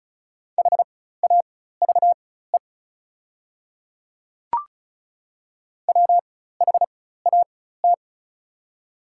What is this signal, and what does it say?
Morse code, heard as a keyed tone, about 36 wpm, spaced as for 16 wpm, 700 Hz. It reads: HAVE WHAT
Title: 4x Character Spacing - Top 100 Words - 35wpm